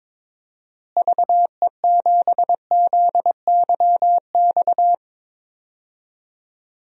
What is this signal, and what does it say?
Morse code, heard as a keyed tone, about 22 wpm, 700 Hz. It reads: VE7ZYX
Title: DXCC-100 Call Signs - 6 Characters - Repeated 2x - 22wpm